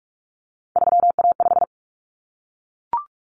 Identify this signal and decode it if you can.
Morse code, heard as a keyed tone, about 44 wpm, 700 Hz. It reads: 3A5
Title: Common Cut Numbers - 3-Digit 45wpm